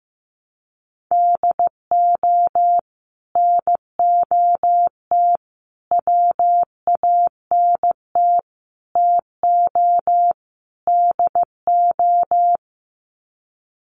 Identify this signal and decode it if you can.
Morse code, heard as a keyed tone, about 15 wpm, 700 Hz. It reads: DO NOT WANT TO DO
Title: Sets of 5 Words 15wpm